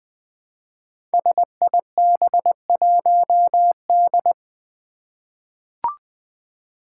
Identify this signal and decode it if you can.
Morse code, heard as a keyed tone, about 20 wpm, 700 Hz. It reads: SIB1D
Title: Five Challenging Letter-Number Combinations 20wpm